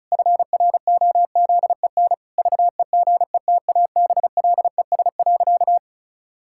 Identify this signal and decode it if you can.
Morse code, heard as a keyed tone, about 36 wpm, 700 Hz. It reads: FROZEN VEGETABLES.
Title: Pangram Challenge v2 - 35wpm